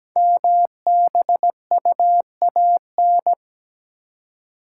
Morse code, heard as a keyed tone, about 17 wpm, 700 Hz. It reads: MBUAN
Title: Code Groups - 5 Letters - 17wpm